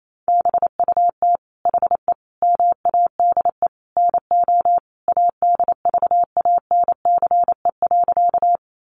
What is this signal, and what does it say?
Morse code, heard as a keyed tone, about 28 wpm, 700 Hz. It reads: BUT HE MADE NO ADVANCE.
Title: Sentences from Top 1400 Words 28wpm